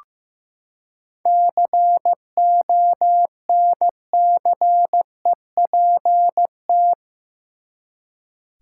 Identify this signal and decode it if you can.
Morse code, heard as a keyed tone, about 15 wpm, 700 Hz. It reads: CONCEPT
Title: Seven Letter Words - 15wpm